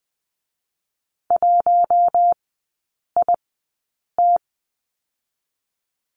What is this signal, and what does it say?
Morse code, heard as a keyed tone, about 20 wpm, spaced as for 8 wpm, 700 Hz. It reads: 1IT
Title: 127 Lesson 21e - Number 6 - Three Characters 20wpm